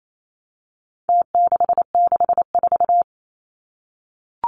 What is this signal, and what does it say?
Morse code, heard as a keyed tone, about 28 wpm, 700 Hz. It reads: T664
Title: Numbers - Contest Style - 4-Digit 28wpm